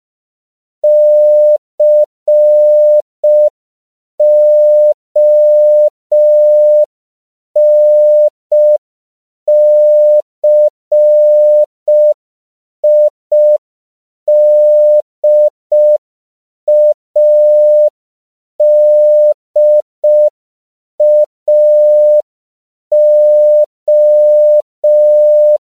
Morse code, heard as a keyed tone, about 5 wpm, 600 Hz. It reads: CONCIDADAO
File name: CW / 5PPM